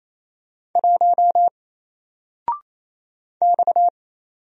Morse code, heard as a keyed tone, about 27 wpm, 700 Hz. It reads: 1 X
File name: Single-Letter-Number-28wpm